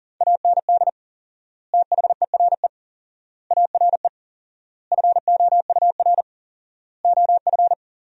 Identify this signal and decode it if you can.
Morse code, heard as a keyed tone, about 40 wpm, 700 Hz. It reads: AND THERE ARE FOUR OF
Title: 4x Word-Spacing - Sentences from Top 300 Words - 40wpm